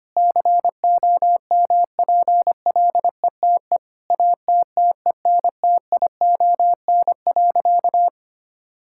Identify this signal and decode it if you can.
Morse code, heard as a keyed tone, about 25 wpm, 700 Hz. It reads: COMPLETE ATTENTION.
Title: Sentences from Top 1000 Words - Review Part 6 - 25wpm